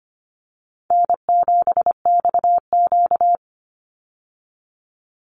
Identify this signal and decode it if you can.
Morse code, heard as a keyed tone, about 25 wpm, 700 Hz. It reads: N7XQ
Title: USA Call Signs 25wpm